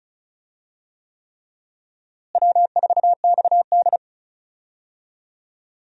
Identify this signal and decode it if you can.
Morse code, heard as a keyed tone, about 36 wpm, 700 Hz. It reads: W4XD
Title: USA Call Signs - 4 Characters - Repeated 3x - 35wpm